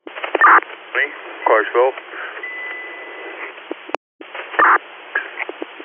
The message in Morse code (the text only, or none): none